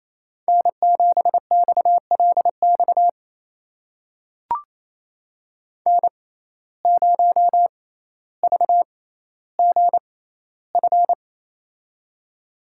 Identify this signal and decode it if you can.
Morse code, heard as a keyed tone, about 28 wpm, spaced as for 14 wpm, 700 Hz. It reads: N7XLX N0VGF
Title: USA Call Signs - 5 Characters - 6x Character Spacing - 28wpm